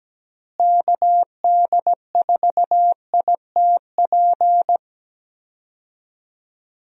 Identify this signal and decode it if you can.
Morse code, heard as a keyed tone, about 17 wpm, 700 Hz. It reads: KD4ITP
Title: USA Call Signs - 6 Characters - Repeated 2x - 17wpm